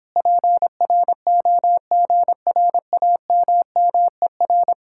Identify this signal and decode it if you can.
Morse code, heard as a keyed tone, about 29 wpm, 700 Hz. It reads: PROGRAMMER